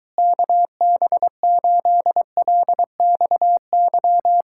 Morse code, heard as a keyed tone, about 23 wpm, 700 Hz. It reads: KB8LXY